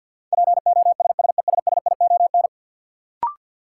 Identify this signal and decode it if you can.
Morse code, heard as a keyed tone, about 46 wpm, 700 Hz. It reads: POSSESSION
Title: Sentences from Top 1400 Words - New Word Drill - 50wpm